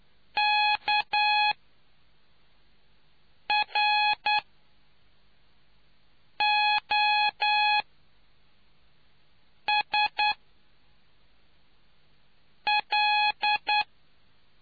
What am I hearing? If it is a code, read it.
Morse code, heard as a keyed tone, about 10 wpm, spaced as for 4 wpm, 840 Hz. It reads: KROSL